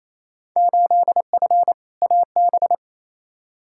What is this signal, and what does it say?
Morse code, heard as a keyed tone, about 28 wpm, 700 Hz. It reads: 8F AB
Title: Field Day Exchanges Encore - 28wpm